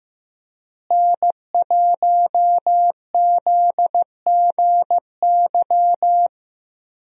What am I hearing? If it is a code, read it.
Morse code, heard as a keyed tone, about 15 wpm, 700 Hz. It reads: N1ZGY
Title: USA Call Signs - 5 Characters - Repeated 2x - 15wpm